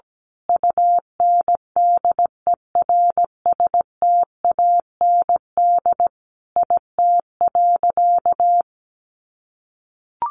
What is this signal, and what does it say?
Morse code, heard as a keyed tone, about 17 wpm, 700 Hz. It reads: UNDERSTAND IT.
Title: Sentences from Top 1400 Words 17wpm